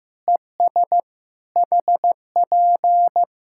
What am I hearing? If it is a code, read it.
Morse code, heard as a keyed tone, about 15 wpm, 700 Hz. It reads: ES HP
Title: Practice QSO - Exchange 4 - Ending - 15wpm